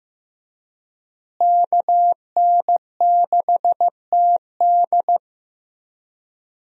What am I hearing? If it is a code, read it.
Morse code, heard as a keyed tone, about 15 wpm, 700 Hz. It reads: KN6TD
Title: USA Call Signs - 5 Characters - Repeated 2x - 15wpm